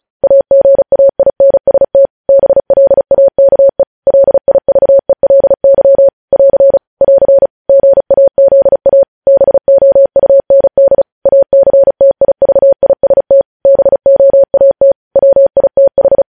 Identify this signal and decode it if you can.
Morse code, heard as a keyed tone, about 36 wpm, 550 Hz. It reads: AGAINST BLAKE LIVELY <AR> <AR> GAZA BOUND ACTIVIST BOAT WITH
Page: news headlines in morse code at 35 wpm